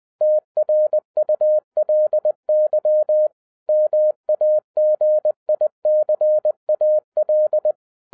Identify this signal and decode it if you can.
Morse code, heard as a keyed tone, about 20 wpm, 610 Hz. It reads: TRULY MAGICAL